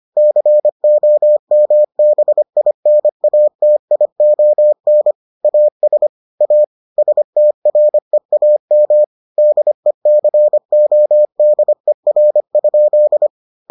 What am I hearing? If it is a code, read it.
Morse code, heard as a keyed tone, about 25 wpm, 600 Hz. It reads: COMBINATION AS A STREAM DECODER?